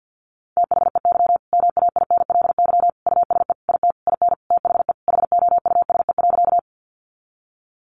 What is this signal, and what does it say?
Morse code, heard as a keyed tone, about 50 wpm, 700 Hz. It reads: THEY MAINLY USE IT IN THE HOUSE.
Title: Sentences from Top 1400 Words 50wpm